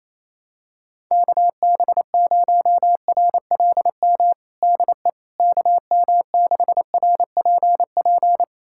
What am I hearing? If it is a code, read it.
Morse code, heard as a keyed tone, about 27 wpm, 700 Hz. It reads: KB0RLM DE KM6RPP
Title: USA Callsign DE Callsign - 6 Characters - 28wpm